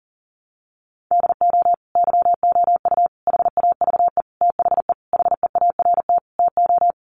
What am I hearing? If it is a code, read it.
Morse code, heard as a keyed tone, about 37 wpm, 700 Hz. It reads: DO YOU HAVE THE HEART TO
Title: Sentences from Top 1200 Words 40wpm